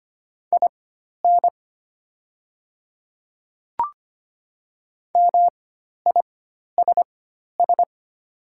Morse code, heard as a keyed tone, about 25 wpm, spaced as for 12 wpm, 690 Hz. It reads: IN MISS
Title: 4x Character Spacing - Top 500 Words - 25wpm